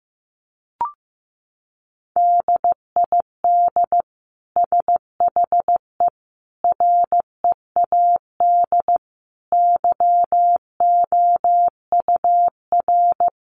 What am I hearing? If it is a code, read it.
Morse code, heard as a keyed tone, about 15 wpm, 700 Hz. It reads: DID SHE READ YOUR
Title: Sentences from Top 1400 Words 15wpm